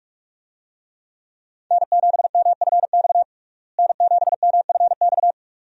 Morse code, heard as a keyed tone, about 46 wpm, 700 Hz. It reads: N7MFX N7MFX
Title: USA Call Signs - 5 Characters - Repeated 2x - 45wpm